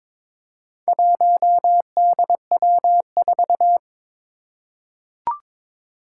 Morse code, heard as a keyed tone, about 22 wpm, 700 Hz. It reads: 1DW4
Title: Four Challenging Letter-Number Combinations 22wpm